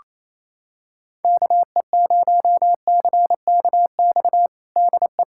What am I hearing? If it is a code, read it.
Morse code, heard as a keyed tone, about 28 wpm, 700 Hz. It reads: KE0CKX DE
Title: USA Callsign DE Callsign - 6 Characters - 28wpm